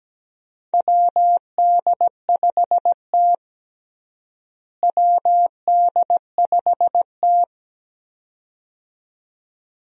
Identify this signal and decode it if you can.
Morse code, heard as a keyed tone, about 17 wpm, 700 Hz. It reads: WD5T WD5T
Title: USA Call Signs - 4 Characters - Repeated 2x - 17wpm